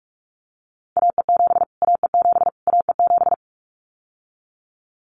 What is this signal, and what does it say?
Morse code, heard as a keyed tone, about 44 wpm, 700 Hz. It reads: AE7 AE7 AE7
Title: Callsign Prefixes for Top 25 DXCC - Repeated 3x - 45wpm